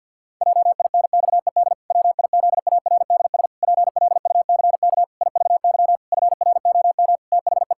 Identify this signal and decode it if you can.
Morse code, heard as a keyed tone, about 50 wpm, 700 Hz. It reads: JINXED WIZARDS PLUCK IVY FROM THE
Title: Pangram Challenge v2 - 50wpm